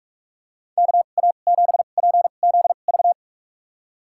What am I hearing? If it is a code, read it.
Morse code, heard as a keyed tone, about 46 wpm, spaced as for 32 wpm, 700 Hz. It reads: KA7PZV
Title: USA Call Signs - 6 Characters - 2x Character Spacing - 45wpm